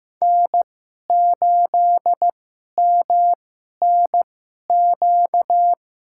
Morse code, heard as a keyed tone, about 15 wpm, spaced as for 11 wpm, 700 Hz. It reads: N8MNQ
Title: USA Call Signs - 5 Characters - 2x Character Spacing - 15wpm